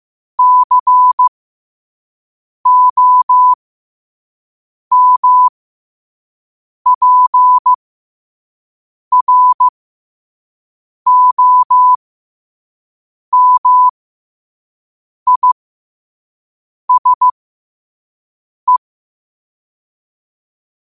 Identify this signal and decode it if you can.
Morse code, heard as a keyed tone, about 15 wpm, spaced as for 5 wpm, 1000 Hz. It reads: COMPROMISE
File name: qotdfiles / 5wpm